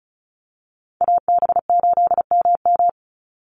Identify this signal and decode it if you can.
Morse code, heard as a keyed tone, about 34 wpm, 700 Hz. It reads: AB8MM
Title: USA Call Signs 35wpm